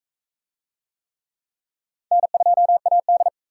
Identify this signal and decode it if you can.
Morse code, heard as a keyed tone, about 40 wpm, 700 Hz. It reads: N2AD